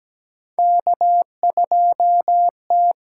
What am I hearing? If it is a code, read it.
Morse code, heard as a keyed tone, about 17 wpm, 700 Hz. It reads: K2T
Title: USA Call Signs - 8x Character Spacing - 17wpm